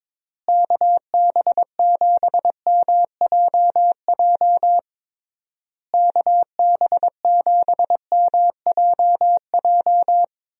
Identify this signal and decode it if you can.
Morse code, heard as a keyed tone, about 22 wpm, 700 Hz. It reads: KB7MJJ KB7MJJ
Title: USA Call Signs - 6 Characters - Repeated 3x - 22wpm